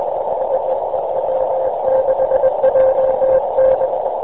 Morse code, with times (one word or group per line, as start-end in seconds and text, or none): none